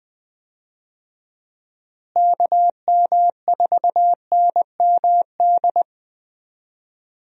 Morse code, heard as a keyed tone, about 20 wpm, 700 Hz. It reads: KM4NMD
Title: USA Call Signs - 6 Characters - Repeated 2x - 20wpm